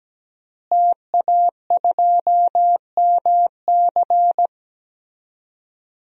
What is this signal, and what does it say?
Morse code, heard as a keyed tone, about 17 wpm, 700 Hz. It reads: TA2MC
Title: DXCC-100 Call Signs - 5 Characters - Repeated 3x - 17wpm